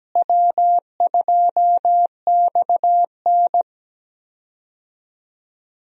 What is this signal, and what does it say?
Morse code, heard as a keyed tone, about 17 wpm, 700 Hz. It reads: W2XN
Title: USA Call Signs - 4 Characters - Repeated 2x - 17wpm